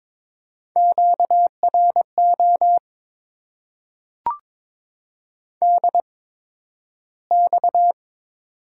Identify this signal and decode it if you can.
Morse code, heard as a keyed tone, about 22 wpm, spaced as for 12 wpm, 700 Hz. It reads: QRO DX
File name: Top 90 QSO Elements - 8x Character Spacing - 22wpm